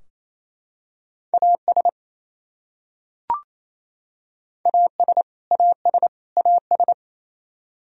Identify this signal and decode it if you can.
Morse code, heard as a keyed tone, about 28 wpm, 700 Hz. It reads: AS AS AS AS
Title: Top 100 Words Repeated 28wpm